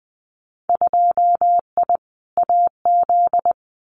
Morse code, heard as a keyed tone, about 19 wpm, 700 Hz. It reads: 2I AZ
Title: Winter Field Day Exchanges 20wpm